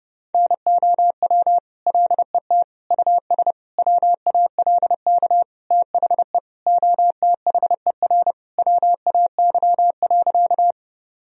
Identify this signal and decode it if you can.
Morse code, heard as a keyed tone, about 32 wpm, 700 Hz. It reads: NOW LET US WALK THE OTHER WAY.
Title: Sentences from Top 1000 Words - Review Part 1 - 30wpm